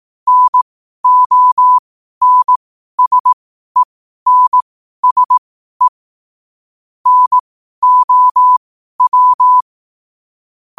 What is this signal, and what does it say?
Morse code, heard as a keyed tone, about 18 wpm, spaced as for 12 wpm, 1000 Hz. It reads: NONSENSE NOW
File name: qotdfiles / 12wpm